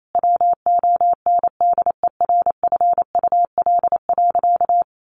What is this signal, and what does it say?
Morse code, heard as a keyed tone, about 28 wpm, 700 Hz. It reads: WONDERFUL.
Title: Sentences from Top 1100 Words 28wpm